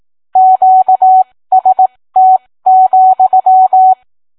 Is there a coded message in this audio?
Yes: Morse code from 0.4 to 3.9 s, heard as a keyed tone, about 18 wpm, spaced as for 15 wpm, 750 Hz. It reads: QST,